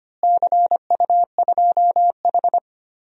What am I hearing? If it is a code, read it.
Morse code, heard as a keyed tone, about 25 wpm, 700 Hz. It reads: CU2H
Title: DXCC-100 Call Signs - 4 Characters - Repeated 3x - 25wpm